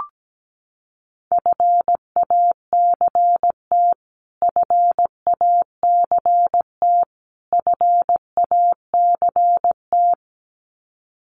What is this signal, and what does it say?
Morse code, heard as a keyed tone, about 17 wpm, 700 Hz. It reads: FACT FACT FACT
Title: Top 500 Words Repeated 17wpm